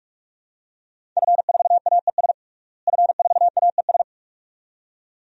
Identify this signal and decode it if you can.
Morse code, heard as a keyed tone, about 43 wpm, 700 Hz. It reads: F4AES F4AES
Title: DXCC-100 Call Signs - 5 Characters - Repeated 2x - 45wpm